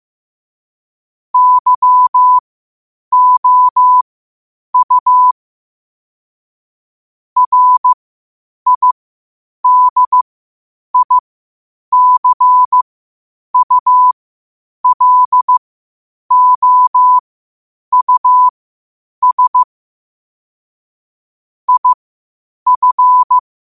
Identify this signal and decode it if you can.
Morse code, heard as a keyed tone, about 15 wpm, spaced as for 8 wpm, 1000 Hz. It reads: YOU RIDICULOUS IF
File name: qotdfiles / 8wpm